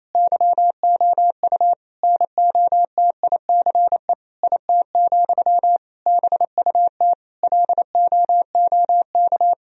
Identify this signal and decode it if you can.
Morse code, heard as a keyed tone, about 27 wpm, 700 Hz. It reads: YOU NOTICE IT, BUT LOOK
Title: Sentences from Top 1000 Words - Review Part 2 - 28wpm